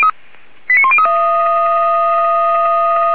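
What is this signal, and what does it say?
Five-tone selective call (ZVEI), heard as quick runs of five tones, tones of about 70 ms. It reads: 89103
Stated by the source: ZVEI five-tone selective call received on an SDR, fire brigade feuerwehr-orlamuende.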